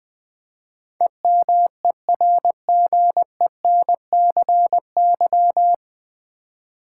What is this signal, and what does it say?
Morse code, heard as a keyed tone, about 21 wpm, 700 Hz. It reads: EMERGENCY
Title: Nine Letter Words - 20wpm